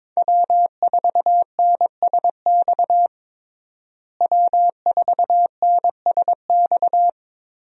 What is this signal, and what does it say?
Morse code, heard as a keyed tone, about 22 wpm, 700 Hz. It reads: W4NSX W4NSX
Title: USA Call Signs - 5 Characters - Repeated 3x - 22wpm